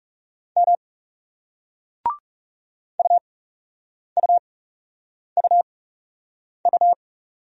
Morse code, heard as a keyed tone, about 40 wpm, spaced as for 8 wpm, 700 Hz. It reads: M UUUU